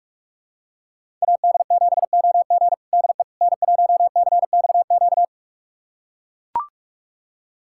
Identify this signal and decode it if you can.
Morse code, heard as a keyed tone, about 41 wpm, 700 Hz. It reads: AD7OG DE N1CXQ
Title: USA Callsign DE Callsign - 5 Characters - 45wpm